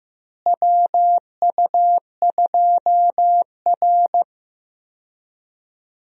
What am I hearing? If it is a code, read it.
Morse code, heard as a keyed tone, about 15 wpm, 700 Hz. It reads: WU2R
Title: USA Call Signs - 4 Characters - Repeated 2x - 15wpm